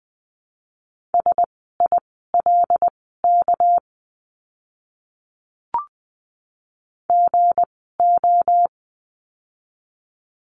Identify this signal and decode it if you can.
Morse code, heard as a keyed tone, about 20 wpm, spaced as for 14 wpm, 700 Hz. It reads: SILK GO
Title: 199 Lesson 32f - Letter k - Words 20wpm